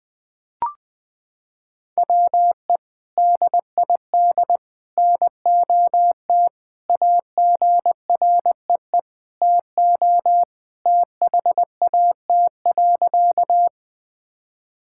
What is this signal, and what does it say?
Morse code, heard as a keyed tone, about 21 wpm, 700 Hz. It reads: WE DID NOT AGREE TO THAT.
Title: Sentences with 3-Word Phrases - 20wpm